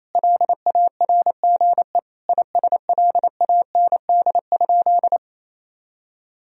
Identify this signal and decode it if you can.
Morse code, heard as a keyed tone, about 28 wpm, 700 Hz. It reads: LARGE ISLAND?
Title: Sentences from Top 1000 Words - Review Part 3 - 28wpm